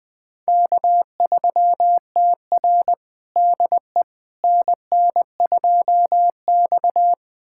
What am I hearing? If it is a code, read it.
Morse code, heard as a keyed tone, about 20 wpm, 700 Hz. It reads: K3TR DE NN2X
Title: USA Callsign DE Callsign - 4 Characters - 20wpm